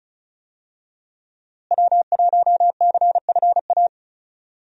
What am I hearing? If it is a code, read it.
Morse code, heard as a keyed tone, about 36 wpm, 700 Hz. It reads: W1CFA
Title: USA Call Signs - 5 Characters - Repeated 3x - 35wpm